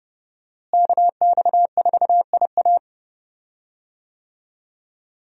Morse code, heard as a keyed tone, about 30 wpm, 700 Hz. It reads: KX4IA
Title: USA Call Signs - 5 Characters - Repeated 3x - 30wpm